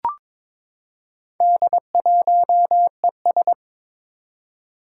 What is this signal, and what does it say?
Morse code, heard as a keyed tone, about 22 wpm, 700 Hz. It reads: D1ES
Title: Four Challenging Letter-Number Combinations 22wpm